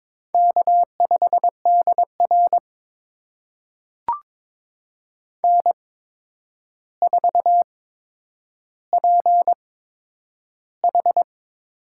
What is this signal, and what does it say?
Morse code, heard as a keyed tone, about 22 wpm, spaced as for 9 wpm, 700 Hz. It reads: K5DR N4PH
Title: USA Call Signs - 4 Characters - 8x Character Spacing - 22wpm